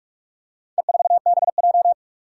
Morse code, heard as a keyed tone, about 44 wpm, 700 Hz. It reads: E4BJ
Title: Four Challenging Letter-Number Combinations 45wpm